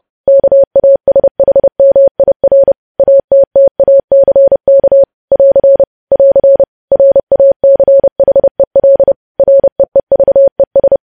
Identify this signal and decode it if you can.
Morse code, heard as a keyed tone, about 30 wpm, 550 Hz. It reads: KASHMIR ATTACK <AR> <AR> RACHEL REEVES
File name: podcast / News headlines-1x-30x30